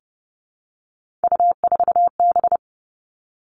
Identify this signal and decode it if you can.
Morse code, heard as a keyed tone, about 30 wpm, 700 Hz. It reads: U4B
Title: Three Challenging Letter-Number Combinations 30wpm